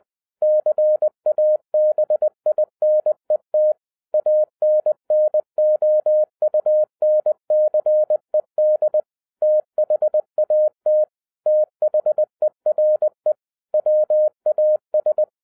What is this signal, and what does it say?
Morse code, heard as a keyed tone, about 20 wpm, 610 Hz. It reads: CABINET ANNOUNCED THAT THERE WAS